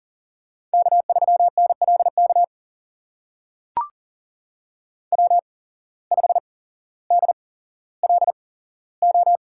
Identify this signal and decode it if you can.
Morse code, heard as a keyed tone, about 42 wpm, spaced as for 17 wpm, 700 Hz. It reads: K3NLK W5DLO
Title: USA Call Signs - 5 Characters - 8x Character Spacing - 40wpm